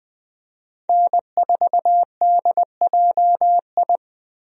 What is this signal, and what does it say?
Morse code, heard as a keyed tone, about 20 wpm, 700 Hz. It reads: N4DJI
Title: USA Call Signs - 5 Characters - Repeated 3x - 20wpm